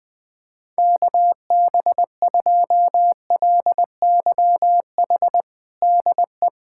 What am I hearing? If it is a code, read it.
Morse code, heard as a keyed tone, about 20 wpm, 700 Hz. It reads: KB2LYH DE
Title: USA Callsign DE Callsign 20wpm